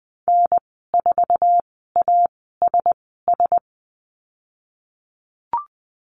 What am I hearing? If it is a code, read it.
Morse code, heard as a keyed tone, about 20 wpm, spaced as for 15 wpm, 700 Hz. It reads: N4ASS
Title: 048 Lesson 9g - Number 4 - Call Signs 20wpm